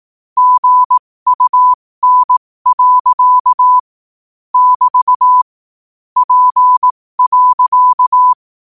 Morse code, heard as a keyed tone, about 18 wpm, spaced as for 15 wpm, 1000 Hz. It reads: GUN. <BT> P.
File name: qotdfiles / 15wpm